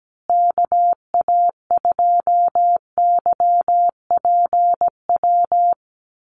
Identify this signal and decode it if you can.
Morse code, heard as a keyed tone, about 17 wpm, 700 Hz. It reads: KA2YPW
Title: USA Call Signs Repeated 3x 17wpm